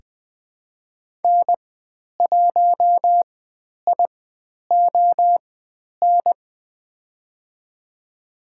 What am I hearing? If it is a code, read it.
Morse code, heard as a keyed tone, about 20 wpm, spaced as for 10 wpm, 700 Hz. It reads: N1ION
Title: 095 Lesson 16g - Letter c - Call Signs 20wpm